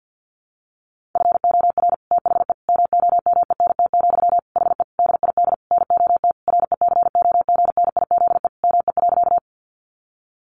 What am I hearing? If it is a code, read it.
Morse code, heard as a keyed tone, about 48 wpm, 700 Hz. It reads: FOR THE MOMENT, HE DID NOT RECOGNIZE ME.
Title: Sentences from Top 1400 Words 50wpm